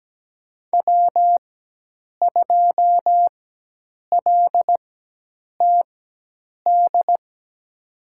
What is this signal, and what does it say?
Morse code, heard as a keyed tone, about 17 wpm, spaced as for 8 wpm, 700 Hz. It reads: W2LTD
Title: USA Call Signs - 5 Characters - 4x Character Spacing - 17wpm